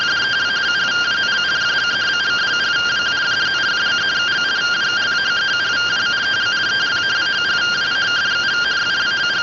Two-tone teleprinter signal, about 170 Hz shift.